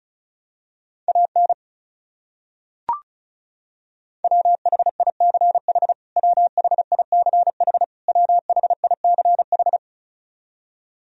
Morse code, heard as a keyed tone, about 36 wpm, 700 Hz. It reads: AN WHICH WHICH WHICH
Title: Top 100 Words Repeated 35wpm